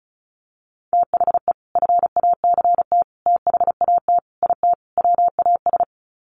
Morse code, heard as a keyed tone, about 34 wpm, 700 Hz. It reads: THE FACT THAT IT WAS
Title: Sets of 5 Words Encore2 35wpm